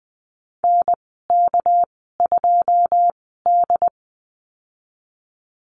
Morse code, heard as a keyed tone, about 20 wpm, spaced as for 15 wpm, 700 Hz. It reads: NK2D
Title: 207 Lesson 33g - Letter j - Call Signs 20wpm